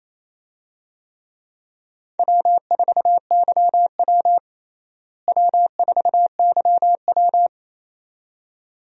Morse code, heard as a keyed tone, about 28 wpm, 700 Hz. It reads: W4YW W4YW
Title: USA Call Signs - 4 Characters - Repeated 2x - 28wpm